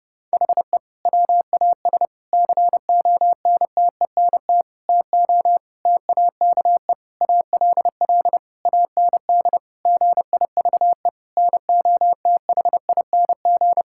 Morse code, heard as a keyed tone, about 30 wpm, 700 Hz. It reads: HE WAS CONTENT TO TAKE ALL AND GIVE NOTHING
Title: Sentences from Top 1000 Words - Review Part 6 - 30wpm